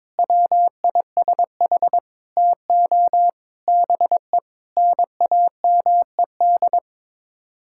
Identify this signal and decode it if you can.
Morse code, heard as a keyed tone, about 22 wpm, 700 Hz. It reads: WISH TO BE NAMED
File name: practice / 22wpm / Sets of 7 Words - 22wpm